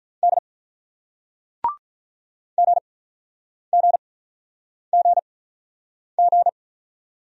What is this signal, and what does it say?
Morse code, heard as a keyed tone, about 46 wpm, spaced as for 9 wpm, 700 Hz. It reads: D GGGG